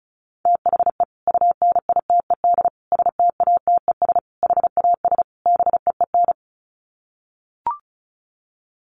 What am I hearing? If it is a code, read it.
Morse code, heard as a keyed tone, about 34 wpm, 700 Hz. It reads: THE UNITED STATES HAS BEEN
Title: Sets of 5 Words Encore2 35wpm